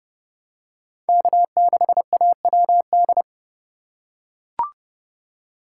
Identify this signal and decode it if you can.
Morse code, heard as a keyed tone, about 29 wpm, 700 Hz. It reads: K6AWD